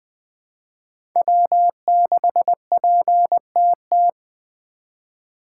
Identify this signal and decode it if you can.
Morse code, heard as a keyed tone, about 21 wpm, 700 Hz. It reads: W6PTT
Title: USA Call Signs - 5 Characters - 3x Character Spacing - 20wpm